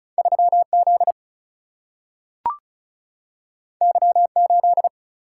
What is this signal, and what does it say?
Morse code, heard as a keyed tone, about 36 wpm, 700 Hz. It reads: 3Z Y8
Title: Single-Letter Single-Number 35wpm